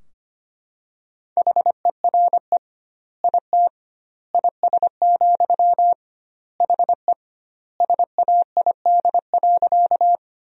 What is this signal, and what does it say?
Morse code, heard as a keyed tone, about 25 wpm, 700 Hz. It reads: HERE IT IS, HE SAID.
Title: 2x Word-Spacing - Sentences from Top 200 Words - 25wpm